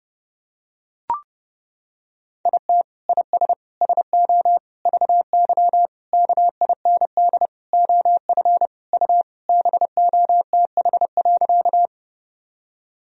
Morse code, heard as a keyed tone, about 30 wpm, 700 Hz. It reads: IT IS SO VY KIND OF U BOTH.
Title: Sentences from Top 300 Words and Abbreviations 30wpm